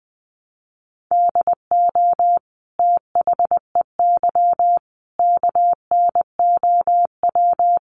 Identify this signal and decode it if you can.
Morse code, heard as a keyed tone, about 20 wpm, 700 Hz. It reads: DO THEY KNOW
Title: Sentences from Top 100 Words 20wpm